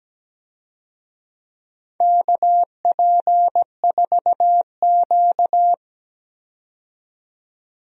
Morse code, heard as a keyed tone, about 17 wpm, 700 Hz. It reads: KP4Q
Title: DXCC-100 Call Signs - 4 Characters - 17wpm